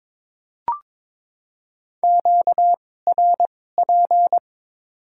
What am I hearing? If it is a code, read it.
Morse code, heard as a keyed tone, about 22 wpm, spaced as for 16 wpm, 700 Hz. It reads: QRP
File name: Top 90 QSO Elements - 2x Character Spacing - 22wpm